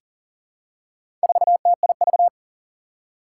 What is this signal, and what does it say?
Morse code, heard as a keyed tone, about 40 wpm, 700 Hz. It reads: 4TIV
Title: Four Challenging Letter-Number Combinations 40wpm